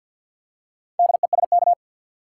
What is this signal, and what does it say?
Morse code, heard as a keyed tone, about 50 wpm, 700 Hz. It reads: DESK
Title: Four Letter Words - 50wpm